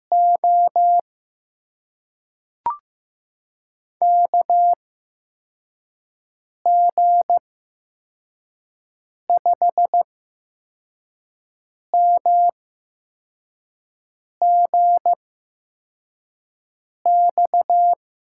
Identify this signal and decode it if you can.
Morse code, heard as a keyed tone, about 15 wpm, spaced as for 4 wpm, 700 Hz. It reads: O KG5MGX
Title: USA Call Signs - 6 Characters - 8x Character Spacing - 15wpm